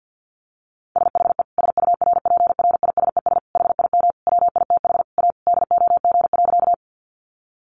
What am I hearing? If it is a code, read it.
Morse code, heard as a keyed tone, about 50 wpm, 700 Hz. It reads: SHE SURPRISES HIM WITH A DOG.
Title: Additional Easy Phrases 50wpm